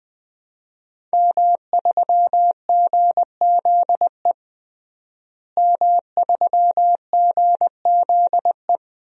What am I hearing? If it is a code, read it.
Morse code, heard as a keyed tone, about 20 wpm, 700 Hz. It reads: M3GZE M3GZE
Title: DXCC-100 Call Signs - 5 Characters - Repeated 2x - 20wpm